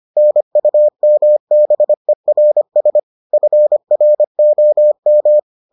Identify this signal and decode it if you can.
Morse code, heard as a keyed tone, about 25 wpm, 600 Hz. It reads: NUMBERS FROM